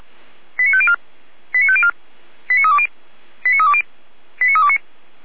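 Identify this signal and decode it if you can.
Five-tone selective call (ZVEI), heard as quick runs of five tones, tones of about 70 ms. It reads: 89504 89504 89320 89320 89329
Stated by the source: ZVEI five-tone selective call received on SDR.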